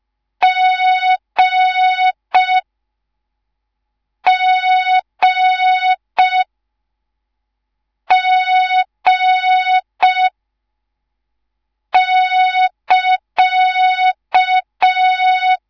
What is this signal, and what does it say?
Morse code, heard as a keyed tone, about 5 wpm, 3700 Hz. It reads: G G G <KA>